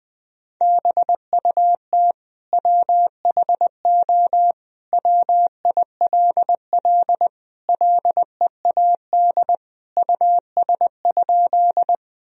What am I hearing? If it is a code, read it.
Morse code, heard as a keyed tone, about 21 wpm, 700 Hz. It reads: BUT WHO WILL LEAD US?
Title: Sentences from Top 1000 Words - Review Part 2 - 20wpm